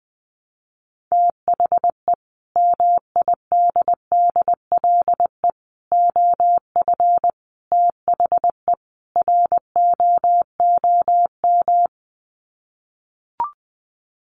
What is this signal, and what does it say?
Morse code, heard as a keyed tone, about 20 wpm, 700 Hz. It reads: THE MIDDLE OF THE ROOM
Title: Sets of 5 Words Encore2 20wpm